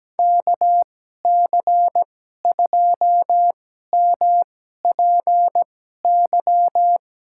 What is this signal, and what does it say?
Morse code, heard as a keyed tone, about 17 wpm, spaced as for 12 wpm, 700 Hz. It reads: KC2MPY
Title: USA Call Signs - 6 Characters - 2x Character Spacing - 17wpm